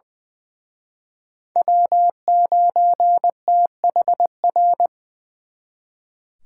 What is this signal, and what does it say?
Morse code, heard as a keyed tone, about 23 wpm, 700 Hz. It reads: W9THR
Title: Speed Racing - USA 1x3 Call Signs 20wpm